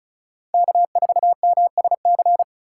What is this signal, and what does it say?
Morse code, heard as a keyed tone, about 36 wpm, 700 Hz. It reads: K4MSC
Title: USA Call Signs - 5 Characters - 6x Character Spacing - 35wpm